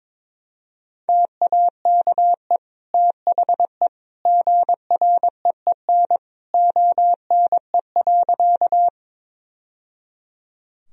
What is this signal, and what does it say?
Morse code, heard as a keyed tone, about 22 wpm, 700 Hz. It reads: TAKE THE GREEN ONE.
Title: Sentences from Top 1000 Words - Review Part 2 - 22wpm